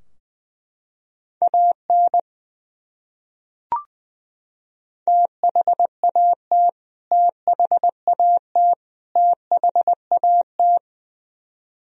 Morse code, heard as a keyed tone, about 21 wpm, 700 Hz. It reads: AN THAT THAT THAT
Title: Top 100 Words Repeated 20wpm